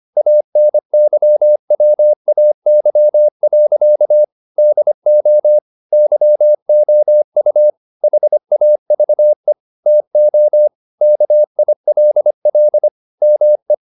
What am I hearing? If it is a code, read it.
Morse code, heard as a keyed tone, about 25 wpm, 600 Hz. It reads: ANYWAY. DO YOU HAVE TO KILL ME